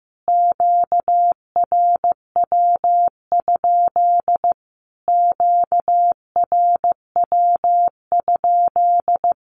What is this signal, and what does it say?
Morse code, heard as a keyed tone, about 15 wpm, 700 Hz. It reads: QRW? QRW?
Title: Q-Codes Repeated 3x 15wpm